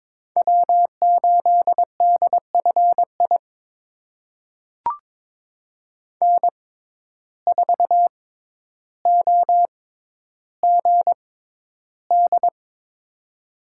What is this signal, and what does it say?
Morse code, heard as a keyed tone, about 23 wpm, spaced as for 11 wpm, 700 Hz. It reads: W8DFI N4OGD
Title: USA Call Signs - 5 Characters - 6x Character Spacing - 22wpm